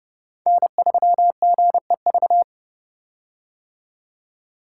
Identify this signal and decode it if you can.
Morse code, heard as a keyed tone, about 32 wpm, 700 Hz. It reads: N3GEV
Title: Code Groups - 5 Letter:Numbers - 30wpm